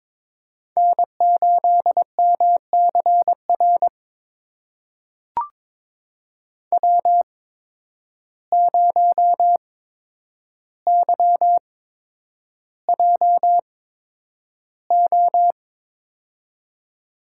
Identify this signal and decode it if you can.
Morse code, heard as a keyed tone, about 22 wpm, spaced as for 9 wpm, 700 Hz. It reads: N8MCR W0YJO
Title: USA Call Signs - 5 Characters - 8x Character Spacing - 22wpm